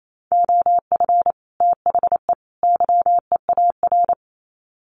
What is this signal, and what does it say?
Morse code, heard as a keyed tone, about 28 wpm, 700 Hz. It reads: OF THE YEAR
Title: Sets of 3 Words 28wpm